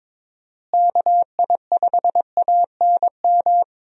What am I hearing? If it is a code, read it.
Morse code, heard as a keyed tone, about 22 wpm, 700 Hz. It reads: KI5ANM
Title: USA Call Signs - 6 Characters - Repeated 3x - 22wpm